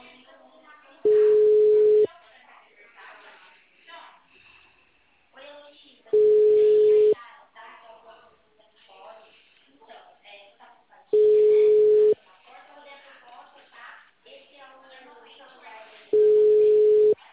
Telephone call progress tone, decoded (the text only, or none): ringback tone